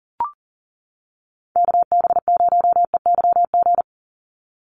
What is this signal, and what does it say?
Morse code, heard as a keyed tone, about 40 wpm, 700 Hz. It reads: KB0EYG
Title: USA Call Signs 40wpm